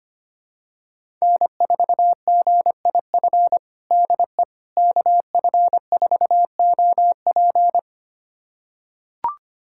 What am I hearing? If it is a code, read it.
Morse code, heard as a keyed tone, about 25 wpm, 700 Hz. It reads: N4GIF DE KF4OP
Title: USA Callsign DE Callsign - 5 Characters - 25wpm